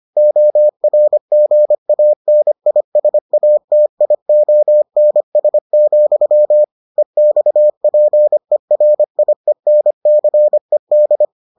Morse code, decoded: ORGANISATIONS, EXPERIENCED

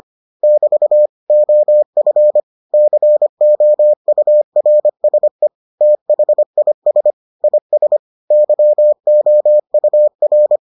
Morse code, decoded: <BT> OF COURSE THIS IS YOUR

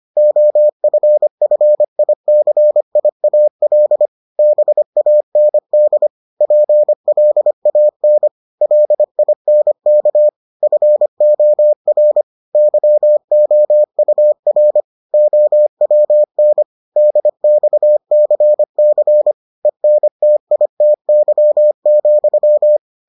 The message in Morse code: OFFICIAL BAND PLAN LINK FOR YOUR OWN DXCC ENTITY,

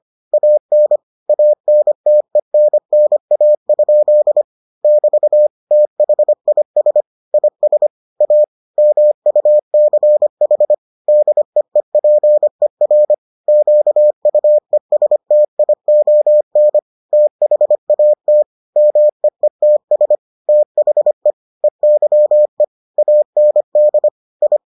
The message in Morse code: AN ANTENNA? <BT> THIS IS A MUCH DEEPER QUESTION THAT MEETS THE EYE AND I

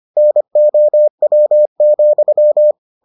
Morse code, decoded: NOW,

